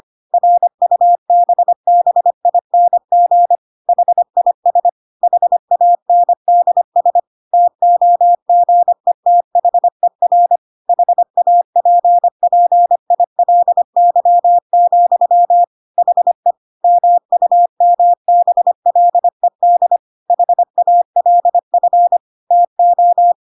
Morse code, heard as a keyed tone, about 25 wpm, 700 Hz. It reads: RUBBING HIS HANDS TOGETHER HAPPILY, HE MUMBLED HALF TO
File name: Pinocchio / Pinocchio ch1 25 25 ik1hge